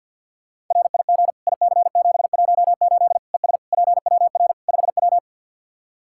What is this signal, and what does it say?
Morse code, heard as a keyed tone, about 52 wpm, 700 Hz. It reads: RIG IC718 ES PWR 5W
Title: K1USN SST QSO Practice 50wpm